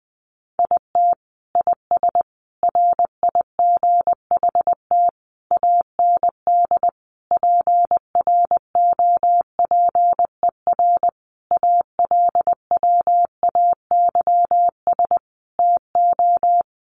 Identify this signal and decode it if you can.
Morse code, heard as a keyed tone, about 19 wpm, 700 Hz. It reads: IT IS RIGHT AND PROPER ALWAYS TO
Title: Sentences from Top 1500 Words 20wpm